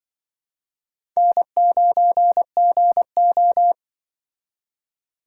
Morse code, heard as a keyed tone, about 23 wpm, 700 Hz. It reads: N9GO
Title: Speed Racing - USA Call Signs 20wpm